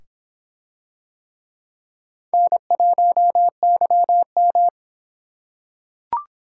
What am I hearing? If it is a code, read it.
Morse code, heard as a keyed tone, about 26 wpm, 700 Hz. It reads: N1YM